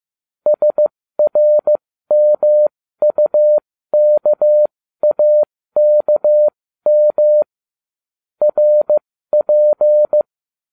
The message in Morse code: SRMUKAKM RP

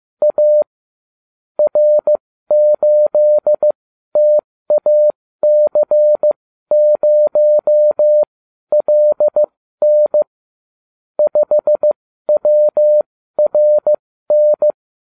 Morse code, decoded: A R8TAC0LN 5WRN

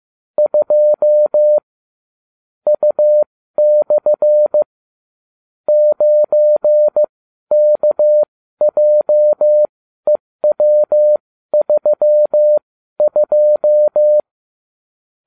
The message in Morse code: 2 U/ 9KJEW32

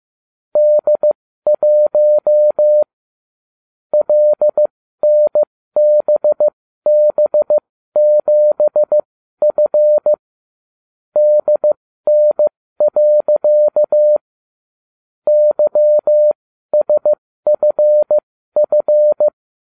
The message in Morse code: D1 LNBB7F DN. YSFF

